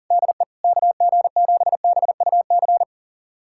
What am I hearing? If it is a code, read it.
Morse code, heard as a keyed tone, about 40 wpm, 700 Hz. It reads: DE KG7BUC
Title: USA Callsign DE Callsign 40wpm